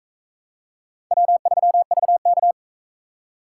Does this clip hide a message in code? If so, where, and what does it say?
Yes: Morse code from 1.1 to 2.5 s, heard as a keyed tone, about 40 wpm, 700 Hz. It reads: W3VK